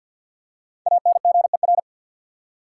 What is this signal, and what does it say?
Morse code, heard as a keyed tone, about 53 wpm, 700 Hz. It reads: ANGER
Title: Five Letter Words - 50wpm